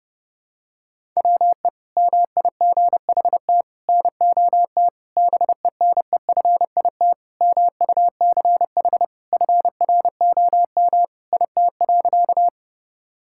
Morse code, heard as a keyed tone, about 32 wpm, 700 Hz. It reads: WE MIGHT NOT BENEFIT MUCH FROM IT.
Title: Sentences from Top 1000 Words - Review Part 6 - 30wpm